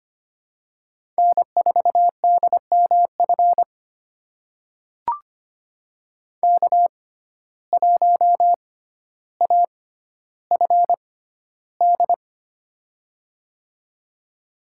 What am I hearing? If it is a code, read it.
Morse code, heard as a keyed tone, about 25 wpm, spaced as for 13 wpm, 690 Hz. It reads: N4DMF K1AFD
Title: USA Call Signs - 5 Characters - 6x Character Spacing - 25wpm